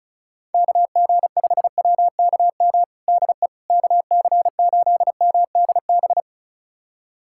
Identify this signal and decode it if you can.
Morse code, heard as a keyed tone, about 36 wpm, 700 Hz. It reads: KG5WKM DE KC8MDB
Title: USA Callsign DE Callsign 35wpm